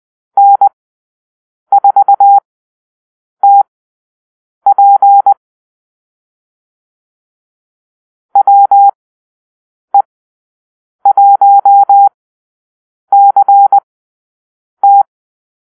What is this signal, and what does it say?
Morse code, heard as a keyed tone, about 20 wpm, spaced as for 7 wpm, 800 Hz. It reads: N4TP WE1CT
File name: arrl / Call-Sign Drill-8